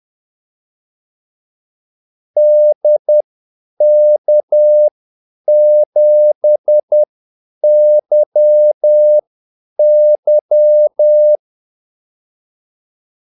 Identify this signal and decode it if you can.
Morse code, heard as a keyed tone, about 10 wpm, spaced as for 8 wpm, 600 Hz. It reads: DK7YY
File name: Callsigns-10-8wpm